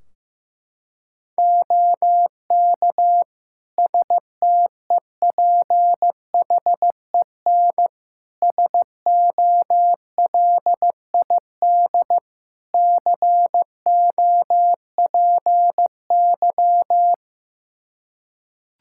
Morse code, heard as a keyed tone, about 16 wpm, 700 Hz. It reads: OK STEPHEN SOLID COPY
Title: K1USN SST QSO Practice 15wpm